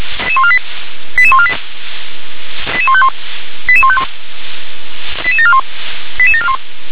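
Five-tone selective call (ZVEI), heard as quick runs of five tones, tones of about 70 ms. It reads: 88148 88151 88151 88741 88741
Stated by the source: SDR-received ZVEI five-tone selective call, address 88126.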